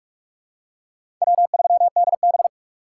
Morse code, heard as a keyed tone, about 42 wpm, 700 Hz. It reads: W3DB
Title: USA Call Signs - 4 Characters - 6x Character Spacing - 45wpm